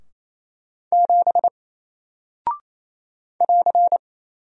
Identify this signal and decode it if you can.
Morse code, heard as a keyed tone, about 27 wpm, 700 Hz. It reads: 7 <AR>
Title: Single Character (Letter:Number:Punctuation:Prosign) 28wpm